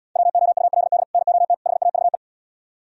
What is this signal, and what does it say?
Morse code, heard as a keyed tone, about 79 wpm, 700 Hz. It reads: WOULD NOT HAVE